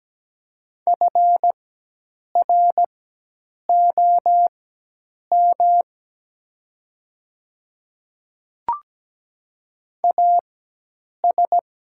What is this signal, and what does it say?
Morse code, heard as a keyed tone, about 17 wpm, spaced as for 8 wpm, 700 Hz. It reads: FROM AS
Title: 4x Character Spacing - Top 100 Words - 17wpm